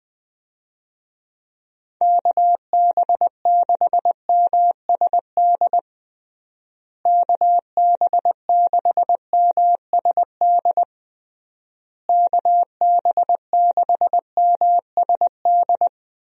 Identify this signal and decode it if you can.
Morse code, heard as a keyed tone, about 20 wpm, 700 Hz. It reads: KB6MSD KB6MSD KB6MSD
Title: USA Call Signs - 6 Characters - Repeated 3x - 20wpm